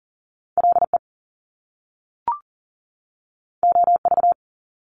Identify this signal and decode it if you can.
Morse code, heard as a keyed tone, about 40 wpm, 700 Hz. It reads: LE OV
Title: Top Two Letter Combinations in Words 40wpm